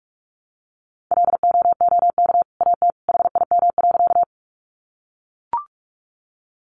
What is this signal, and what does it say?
Morse code, heard as a keyed tone, about 44 wpm, 700 Hz. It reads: LOOK AT HIM.
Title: Sentences from Top 100 Words 45wpm